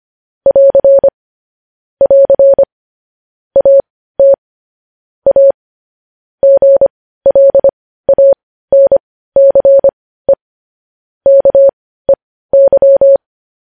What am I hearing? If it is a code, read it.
Morse code, heard as a keyed tone, about 25 wpm, spaced as for 15 wpm, 550 Hz. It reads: <AR> <AR> AT A GLANCE KEY
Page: news headlines in morse code at 15 wpm